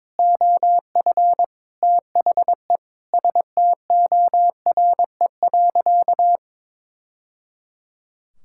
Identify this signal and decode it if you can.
Morse code, heard as a keyed tone, about 22 wpm, 700 Hz. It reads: OF THE STORE.
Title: Sentences from Top 1000 Words - Review Part 5 - 22wpm